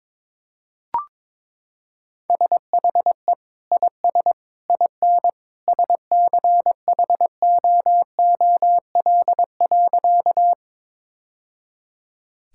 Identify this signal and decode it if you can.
Morse code, heard as a keyed tone, about 22 wpm, 700 Hz. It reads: SHE IS IN SCHOOL.
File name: practice / 22wpm / Sentences with 2-Word Phrases - 22wpm